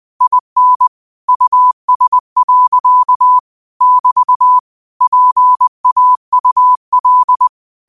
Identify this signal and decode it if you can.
Morse code, heard as a keyed tone, about 20 wpm, 1000 Hz. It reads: IN US. <BT> PAUL
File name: qotdfiles / 20wpm